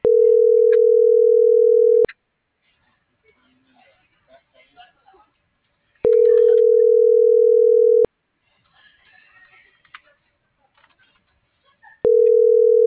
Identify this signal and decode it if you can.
Ringback tone (North America), 440 + 480 Hz.